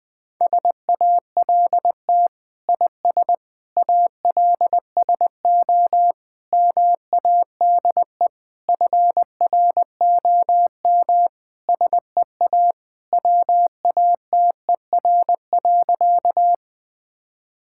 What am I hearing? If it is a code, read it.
Morse code, heard as a keyed tone, about 21 wpm, 700 Hz. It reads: SALT IS ALSO MADE FROM SEA WATER.
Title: Sentences from Top 1000 Words - Review Part 5 - 20wpm